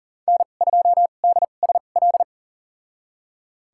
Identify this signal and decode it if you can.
Morse code, heard as a keyed tone, about 40 wpm, spaced as for 29 wpm, 700 Hz. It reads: N2DSL
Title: USA Call Signs - 5 Characters - 2x Character Spacing - 40wpm